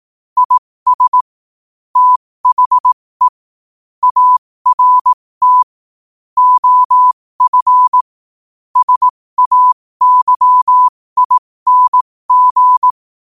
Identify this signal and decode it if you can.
Morse code, heard as a keyed tone, about 18 wpm, spaced as for 15 wpm, 1000 Hz. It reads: IS THE ART OF SAYING
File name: qotdfiles / 15wpm